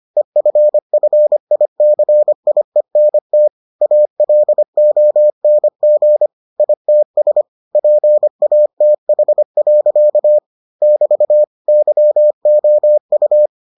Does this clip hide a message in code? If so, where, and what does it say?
Yes: Morse code from 0.2 to 13.5 s, heard as a keyed tone, about 25 wpm, 600 Hz. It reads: EFFICIENT ALONG ITS PATH. <BT> YOU